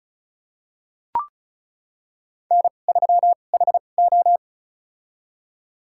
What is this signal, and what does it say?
Morse code, heard as a keyed tone, about 36 wpm, spaced as for 25 wpm, 700 Hz. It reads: N3HO
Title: USA Call Signs - 4 Characters - 2x Character Spacing - 35wpm